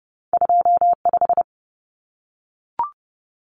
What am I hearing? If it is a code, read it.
Morse code, heard as a keyed tone, about 30 wpm, 700 Hz. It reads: 25
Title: Common Cut Numbers - 2-Digit 30wpm